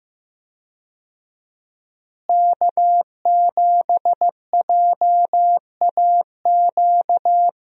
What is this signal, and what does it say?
Morse code, heard as a keyed tone, about 15 wpm, 700 Hz. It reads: K7JAQ
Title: USA Call Signs - 5 Characters - Repeated 2x - 15wpm